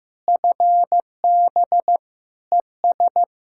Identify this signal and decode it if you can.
Morse code, heard as a keyed tone, about 15 wpm, 700 Hz. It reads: FB ES
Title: Practice QSO - Exchange 2 - Info - 15wpm